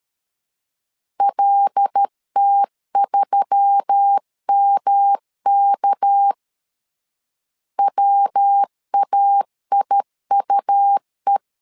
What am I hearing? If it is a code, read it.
Morse code, heard as a keyed tone, about 13 wpm, 800 Hz. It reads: LT3MK WAIUE
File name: CW / DET / 1 6 1500C S C 12wpm 3533